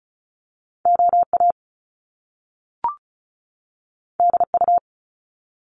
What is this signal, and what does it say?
Morse code, heard as a keyed tone, about 36 wpm, 700 Hz. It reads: OA DU
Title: Top Two Letter Combinations in Words 35wpm